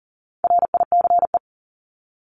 Morse code, heard as a keyed tone, about 40 wpm, 700 Hz. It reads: RICE
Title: Top 1501-1750 Words 40wpm